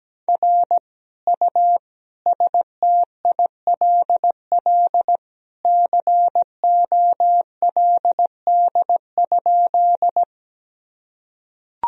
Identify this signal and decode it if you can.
Morse code, heard as a keyed tone, about 17 wpm, 700 Hz. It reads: R U STILL COLD?
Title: Sentences from Top 400 Words and Abbreviations 17wpm